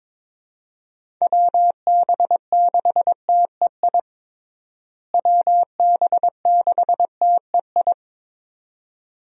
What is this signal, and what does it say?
Morse code, heard as a keyed tone, about 22 wpm, 700 Hz. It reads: WB6TEI WB6TEI
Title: USA Call Signs - 6 Characters - Repeated 2x - 22wpm